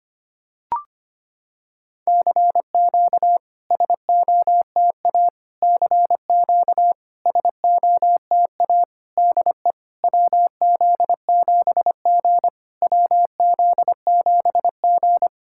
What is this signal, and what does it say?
Morse code, heard as a keyed tone, about 25 wpm, 700 Hz. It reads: CQ SOTA CQ SOTA DE WZ7G WZ7G
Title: SOTA QSO Practice 25wpm